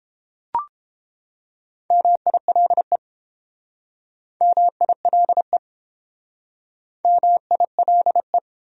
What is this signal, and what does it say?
Morse code, heard as a keyed tone, about 30 wpm, 700 Hz. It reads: MILE MILE MILE